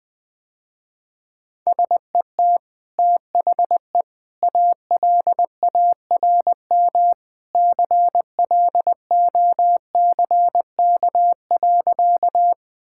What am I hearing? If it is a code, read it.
Morse code, heard as a keyed tone, about 20 wpm, 700 Hz. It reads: SET THE ALARM CLOCK.
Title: Easy Sentences with Hints - 20wpm